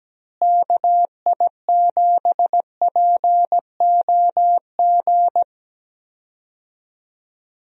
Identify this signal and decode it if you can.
Morse code, heard as a keyed tone, about 17 wpm, 700 Hz. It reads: KI7POG
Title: USA Call Signs - 6 Characters - Repeated 3x - 17wpm